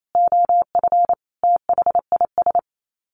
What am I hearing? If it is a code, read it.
Morse code, heard as a keyed tone, about 28 wpm, 700 Hz. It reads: OF THIS
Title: Sets of 5 Words Encore 28wpm